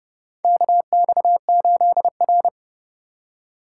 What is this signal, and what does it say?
Morse code, heard as a keyed tone, about 30 wpm, 700 Hz. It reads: KX8R
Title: USA Call Signs - 4 Characters - 3x Character Spacing - 30wpm